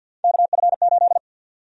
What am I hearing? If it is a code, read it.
Morse code, heard as a keyed tone, about 49 wpm, 700 Hz. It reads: KF8
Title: USA Call Sign Prefixes - 50wpm